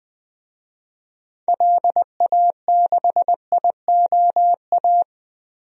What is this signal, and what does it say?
Morse code, heard as a keyed tone, about 21 wpm, 700 Hz. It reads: LA6IOA
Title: DXCC-100 Call Signs - 6 Characters - 20wpm